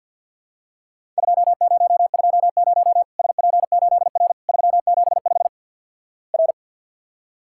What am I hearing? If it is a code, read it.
Morse code, heard as a keyed tone, about 50 wpm, 700 Hz. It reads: 2020 SP8R 375 R
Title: Worked All Europe DX Contest - QTC Exchange - 50wpm